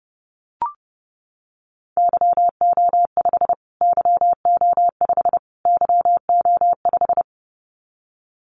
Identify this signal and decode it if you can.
Morse code, heard as a keyed tone, about 30 wpm, 700 Hz. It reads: YO5 YO5 YO5
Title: Callsign Prefixes for Top 25 DXCC - Repeated 3x - 30wpm